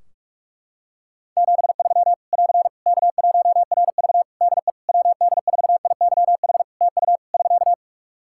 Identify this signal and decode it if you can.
Morse code, heard as a keyed tone, about 46 wpm, 700 Hz. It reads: 73 <AR> K1RV DE WD4IYH TU <SK>
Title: Practice QSO - Exchange 4 - Ending - 45wpm